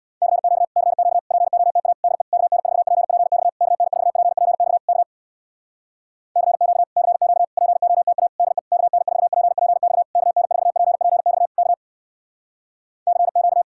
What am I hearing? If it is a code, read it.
Morse code, heard as a keyed tone, about 65 wpm, 700 Hz. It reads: CQ CQ POTA DE KN4PPX KN4PPX K CQ CQ POTA DE KN4PPX KN4PPX K CQ